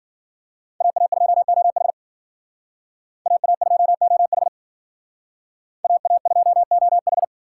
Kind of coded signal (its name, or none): Morse code